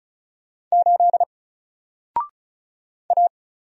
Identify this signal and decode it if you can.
Morse code, heard as a keyed tone, about 35 wpm, 700 Hz. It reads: 8 A
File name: Single-Letter-Number-35wpm